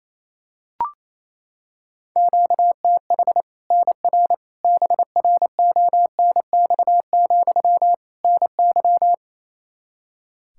Morse code, heard as a keyed tone, about 28 wpm, 700 Hz. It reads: QTH NR BRONX, NY
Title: Practice QSO - Exchange 1 - Beginning - 28wpm